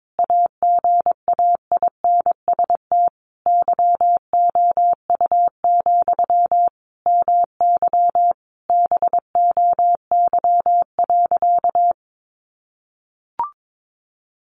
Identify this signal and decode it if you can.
Morse code, heard as a keyed tone, about 22 wpm, 700 Hz. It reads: AGAINST YOU, MY BOY.
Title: Sentences from Top 700 Words 22wpm